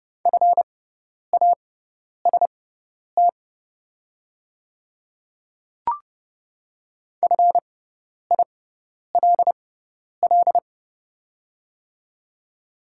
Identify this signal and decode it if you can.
Morse code, heard as a keyed tone, about 31 wpm, spaced as for 10 wpm, 700 Hz. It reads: FAST FILL
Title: 6x Character Spacing - Top 500 Words - 30wpm